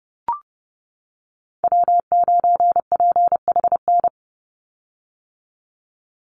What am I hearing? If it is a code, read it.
Morse code, heard as a keyed tone, about 30 wpm, 700 Hz. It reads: W9PHN
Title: USA Call Signs 30wpm